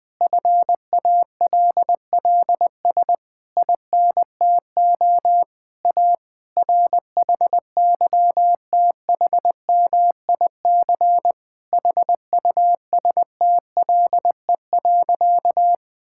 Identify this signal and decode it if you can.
Morse code, heard as a keyed tone, about 21 wpm, 700 Hz. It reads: FALLS INTO A RHYTHMIC HUSTLE.
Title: Difficult Sentences with Hints - 20wpm